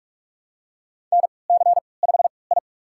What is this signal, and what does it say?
Morse code, heard as a keyed tone, about 44 wpm, spaced as for 25 wpm, 700 Hz. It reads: NC5I
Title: USA Call Signs - 4 Characters - 3x Character Spacing - 45wpm